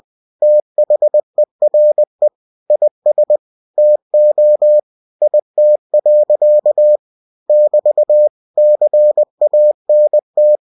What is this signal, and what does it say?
Morse code, heard as a keyed tone, about 20 wpm, 600 Hz. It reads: THERE IS TO IT. <BT> CANT
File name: cw-books / winnie-the-pooh / winnie-20wpm-page-0045